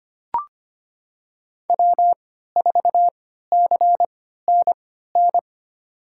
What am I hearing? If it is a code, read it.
Morse code, heard as a keyed tone, about 25 wpm, spaced as for 14 wpm, 700 Hz. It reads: W4CNN
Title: USA Call Signs - 5 Characters - 3x Character Spacing - 25wpm